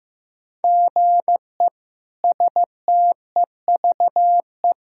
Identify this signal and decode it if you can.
Morse code, heard as a keyed tone, about 15 wpm, 700 Hz. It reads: GE STEVE
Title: K1USN SST QSO Practice - 15wpm